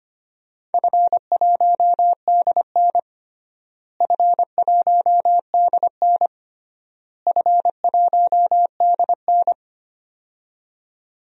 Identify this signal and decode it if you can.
Morse code, heard as a keyed tone, about 25 wpm, 700 Hz. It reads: F1DN F1DN F1DN
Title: DXCC-100 Call Signs - 4 Characters - Repeated 3x - 25wpm